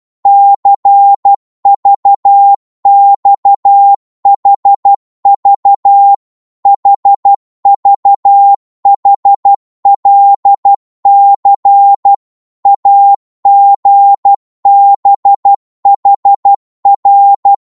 Morse code, heard as a keyed tone, about 12 wpm, 800 Hz. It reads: CVXHV HVHLC AGBHR
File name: Morse / ltr-12wpm